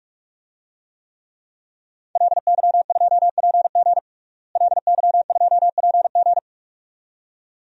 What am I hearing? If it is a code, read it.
Morse code, heard as a keyed tone, about 42 wpm, 700 Hz. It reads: LY2PG LY2PG
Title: DXCC-100 Call Signs - 5 Characters - Repeated 2x - 45wpm